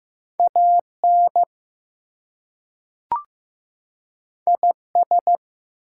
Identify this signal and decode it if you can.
Morse code, heard as a keyed tone, about 15 wpm, 700 Hz. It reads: AN IS
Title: Two Letter Words - 15wpm